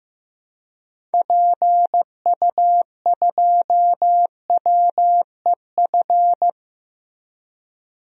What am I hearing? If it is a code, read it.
Morse code, heard as a keyed tone, about 15 wpm, 700 Hz. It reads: PU2WEF
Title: DXCC-100 Call Signs - 6 Characters - Repeated 3x - 15wpm